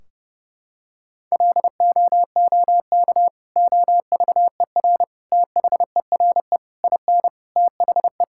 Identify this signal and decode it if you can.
Morse code, heard as a keyed tone, about 32 wpm, 700 Hz. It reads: LOOK OVER THERE IN THE
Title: Sentences from Top 1600 Words 30wpm